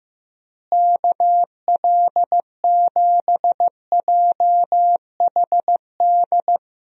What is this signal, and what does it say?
Morse code, heard as a keyed tone, about 14 wpm, 700 Hz. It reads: KL7JHD
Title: USA Call Signs - 6 Characters - Repeated 3x - 15wpm